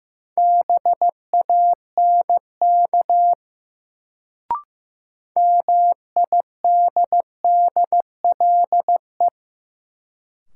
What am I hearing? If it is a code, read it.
Morse code, heard as a keyed tone, about 15 wpm, 700 Hz. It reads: BANK MIDDLE
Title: Sentences from Top 900 Words - New Word Drill - 15wpm